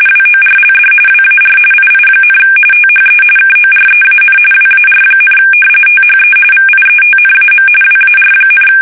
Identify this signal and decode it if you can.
RTTY, about 45.45 baud. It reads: WORLD WAR II, THE KOREAN WAR, THE VIETNAM WAR,
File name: fsk850 / battleshipnj